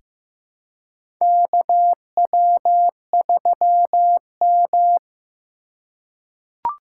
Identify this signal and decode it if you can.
Morse code, heard as a keyed tone, about 15 wpm, 690 Hz. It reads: KW3M
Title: USA Call Signs - 4 Characters - 2x Character Spacing - 15wpm